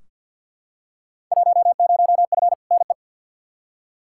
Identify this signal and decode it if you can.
Morse code, heard as a keyed tone, about 46 wpm, 700 Hz. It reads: 10F NE
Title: Field Day Exchanges Encore - 50wpm